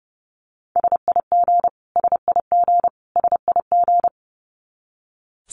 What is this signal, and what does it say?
Morse code, heard as a keyed tone, about 30 wpm, 700 Hz. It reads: SIG SIG SIG
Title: Abbreviations Repeated 3x - 30wpm